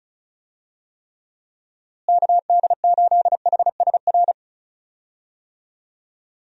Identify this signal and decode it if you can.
Morse code, heard as a keyed tone, about 34 wpm, 700 Hz. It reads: KD8HSR
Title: USA Call Signs - 6 Characters - Repeated 2x - 35wpm